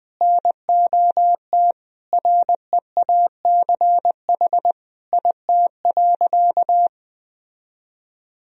Morse code, heard as a keyed tone, about 20 wpm, 700 Hz. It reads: NOT REACH IT.
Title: Sentences from Top 1000 Words - Review Part 2 - 20wpm